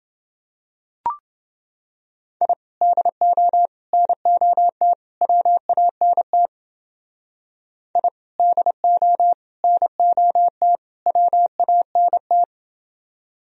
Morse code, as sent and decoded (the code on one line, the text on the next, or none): .. / -.. --- / -. --- - / .-- .- -. - / .. / -.. --- / -. --- - / .-- .- -. -
I DO NOT WANT I DO NOT WANT